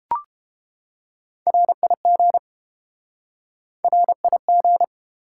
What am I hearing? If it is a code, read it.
Morse code, heard as a keyed tone, about 30 wpm, 700 Hz. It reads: RIG RIG